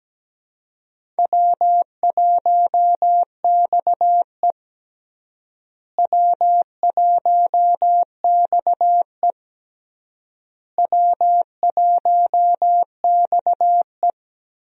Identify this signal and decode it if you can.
Morse code, heard as a keyed tone, about 17 wpm, 700 Hz. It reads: W1XE W1XE W1XE
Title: USA Call Signs - 4 Characters - Repeated 3x - 17wpm